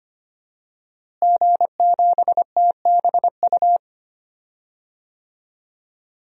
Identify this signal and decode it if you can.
Morse code, heard as a keyed tone, about 25 wpm, 700 Hz. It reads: G7TBU
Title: DXCC-100 Call Signs - 5 Characters - Repeated 3x - 25wpm